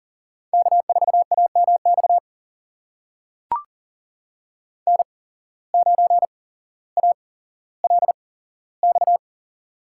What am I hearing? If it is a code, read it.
Morse code, heard as a keyed tone, about 40 wpm, spaced as for 17 wpm, 700 Hz. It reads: K4AMX N9ALX
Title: USA Call Signs - 5 Characters - 8x Character Spacing - 40wpm